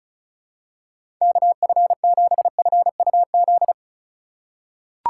Morse code, heard as a keyed tone, about 34 wpm, 700 Hz. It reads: KF7FUZ
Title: USA Call Signs - 6 Characters - 2x Character Spacing - 35wpm